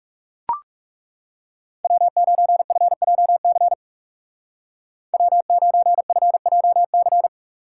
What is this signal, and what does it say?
Morse code, heard as a keyed tone, about 37 wpm, 700 Hz. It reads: W9FJC W9FJC
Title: Speed Racing - USA 1x3 Call Signs 30wpm